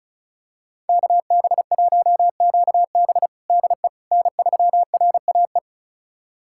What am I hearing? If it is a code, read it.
Morse code, heard as a keyed tone, about 34 wpm, 700 Hz. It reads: KB1QB DE N3RAE
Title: USA Callsign DE Callsign - 5 Characters - 35wpm